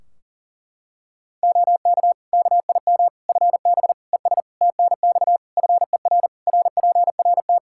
Morse code, heard as a keyed tone, about 37 wpm, 700 Hz. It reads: OK KIM FB ES TNX FER RPRT
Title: Practice QSO - Exchange 2 - Info - 40wpm